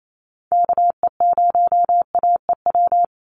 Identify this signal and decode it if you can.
Morse code, heard as a keyed tone, about 28 wpm, 700 Hz. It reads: KE0AEW
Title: USA Call Signs 28wpm